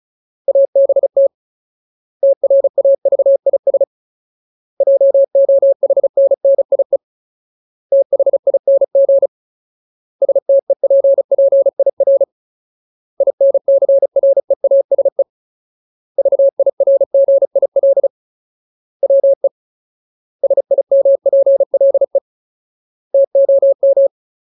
Morse code, heard as a keyed tone, about 34 wpm, 550 Hz. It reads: ABT TRAVIS JOHNNIE THING STEPPIR INCREASE VIRGIL WE SIMPLE TOM
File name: Ultimate Word List / 35WPM Ultimate